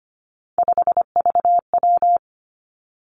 Morse code, heard as a keyed tone, about 25 wpm, 700 Hz. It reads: 5VW
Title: Three Challenging Letter-Number Combinations 25wpm